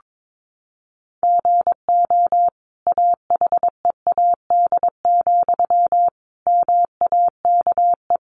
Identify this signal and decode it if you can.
Morse code, heard as a keyed tone, about 22 wpm, 700 Hz. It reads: GO AHEAD, MAKE
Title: Sentences from Top 1100 Words 22wpm